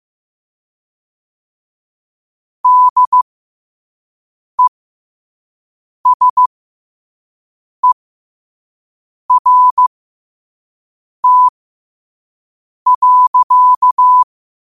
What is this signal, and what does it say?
Morse code, heard as a keyed tone, about 15 wpm, spaced as for 5 wpm, 1000 Hz. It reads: DESERT.
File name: qotdfiles / 5wpm